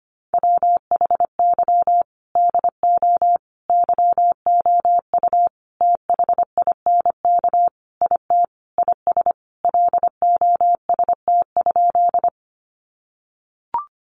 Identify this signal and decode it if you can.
Morse code, heard as a keyed tone, about 25 wpm, 700 Hz. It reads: WHY DO YOU THINK IT IS LOST?
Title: Sentences from Top 1400 Words 25wpm